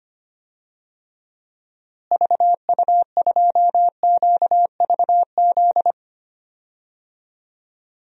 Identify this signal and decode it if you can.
Morse code, heard as a keyed tone, about 25 wpm, 700 Hz. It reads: VU2QVZ
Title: DXCC-100 Call Signs - 6 Characters - Repeated 3x - 25wpm